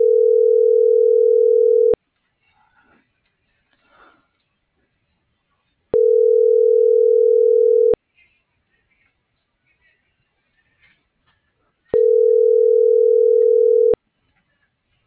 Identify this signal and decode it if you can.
Ringback tone (North America), 440 + 480 Hz.